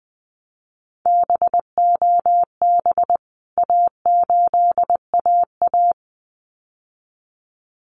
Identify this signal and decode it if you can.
Morse code, heard as a keyed tone, about 20 wpm, 700 Hz. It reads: BOB A8AA
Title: CWT Exchange 20wpm